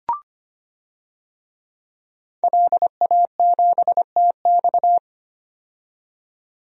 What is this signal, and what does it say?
Morse code, heard as a keyed tone, about 25 wpm, 700 Hz. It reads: LA7TX
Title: DXCC-100 Call Signs - 5 Characters - 25wpm